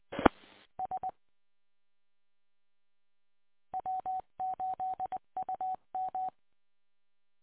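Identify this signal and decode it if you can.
Morse code, heard as a keyed tone, about 20 wpm, 750 Hz. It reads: E S W8UM
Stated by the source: Morse code recorded on the W8UM 145.230 audio archive.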